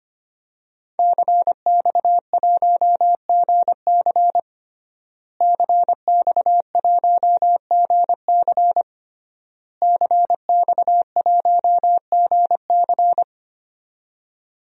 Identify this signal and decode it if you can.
Morse code, heard as a keyed tone, about 25 wpm, 700 Hz. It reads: CX1GC CX1GC CX1GC
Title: DXCC-100 Call Signs - 5 Characters - Repeated 3x - 25wpm